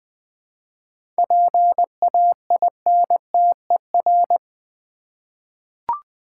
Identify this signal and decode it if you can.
Morse code, heard as a keyed tone, about 20 wpm, 700 Hz. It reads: PAINTER
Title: 7 Letter Words - First 3 Letters Given - 20wpm